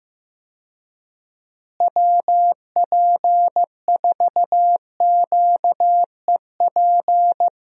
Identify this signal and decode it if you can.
Morse code, heard as a keyed tone, about 15 wpm, 700 Hz. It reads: WP4QEP
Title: USA Call Signs - 6 Characters - Repeated 3x - 15wpm